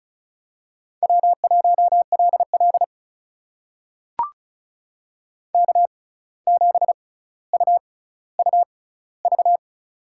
Morse code, heard as a keyed tone, about 36 wpm, spaced as for 17 wpm, 700 Hz. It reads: W1LL K7UUV
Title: USA Call Signs - 6x Character Spacing - 35wpm